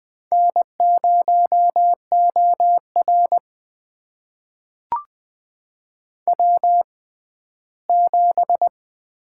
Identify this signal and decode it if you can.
Morse code, heard as a keyed tone, about 20 wpm, spaced as for 14 wpm, 700 Hz. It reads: N0OR W7
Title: USA Call Signs - 6x Character Spacing - 20wpm